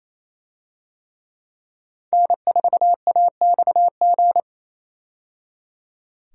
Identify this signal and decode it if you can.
Morse code, heard as a keyed tone, about 27 wpm, 700 Hz. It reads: N4AXG
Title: USA Call Signs - 5 Characters - 28wpm